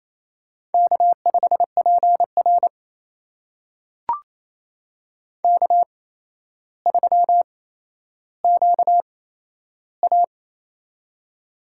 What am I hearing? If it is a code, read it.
Morse code, heard as a keyed tone, about 27 wpm, spaced as for 12 wpm, 700 Hz. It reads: K5PR K3QA
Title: USA Call Signs - 4 Characters - 8x Character Spacing - 28wpm